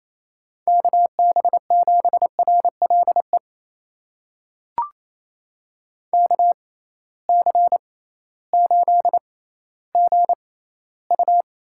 Morse code, heard as a keyed tone, about 28 wpm, spaced as for 15 wpm, 700 Hz. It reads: KB7RLE KC8GU
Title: USA Call Signs - 6 Characters - 6x Character Spacing - 28wpm